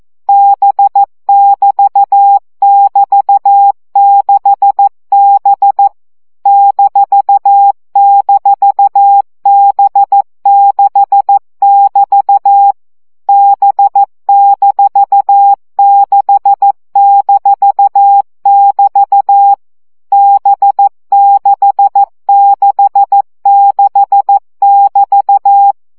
Morse code, heard as a keyed tone, about 14 wpm, 800 Hz. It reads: B<BT><BT>6B --B6<BT> B-6-<BT> B666<BT>
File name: CW gyakorlas b 6 egyenlo minusz 12 baud